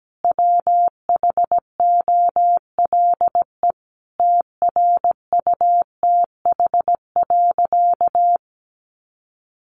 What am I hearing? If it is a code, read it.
Morse code, heard as a keyed tone, about 17 wpm, 700 Hz. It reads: WHOLE TRUTH.
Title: Sentences from Top 800 Words 17wpm